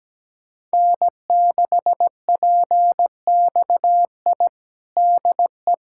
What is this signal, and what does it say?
Morse code, heard as a keyed tone, about 17 wpm, 700 Hz. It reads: N6PXI DE
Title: USA Callsign DE Callsign 17wpm